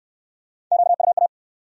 Morse code, heard as a keyed tone, about 67 wpm, 700 Hz. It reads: QUA